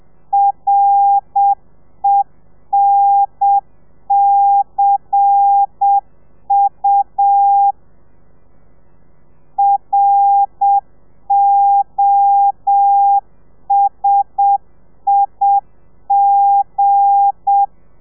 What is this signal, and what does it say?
Morse code, heard as a keyed tone, about 7 wpm, 800 Hz. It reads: RENCU ROSIG
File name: telegrafia / i2tfj / Lez8 30